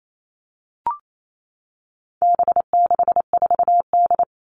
Morse code, heard as a keyed tone, about 28 wpm, 700 Hz. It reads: B64D
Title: All Cut Numbers - 4-Digit 28wpm